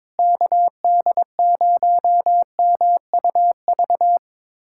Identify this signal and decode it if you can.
Morse code, heard as a keyed tone, about 21 wpm, 700 Hz. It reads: KD0MUV
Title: USA Callsign DE Callsign 22wpm